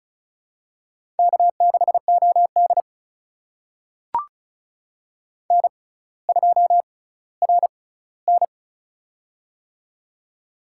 Morse code, heard as a keyed tone, about 36 wpm, spaced as for 18 wpm, 700 Hz. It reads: K6OD N2RN
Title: USA Call Signs - 4 Characters - 6x Character Spacing - 35wpm